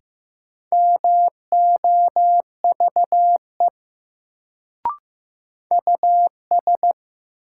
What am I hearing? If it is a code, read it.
Morse code, heard as a keyed tone, about 15 wpm, 700 Hz. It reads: MOVE US
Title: Sentences from Top 200 Words - New Word Drill - 15wpm